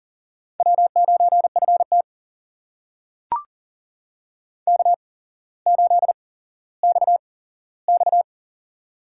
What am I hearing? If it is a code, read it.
Morse code, heard as a keyed tone, about 39 wpm, spaced as for 17 wpm, 700 Hz. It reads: W9FT K8XX
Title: USA Call Signs - 4 Characters - 8x Character Spacing - 40wpm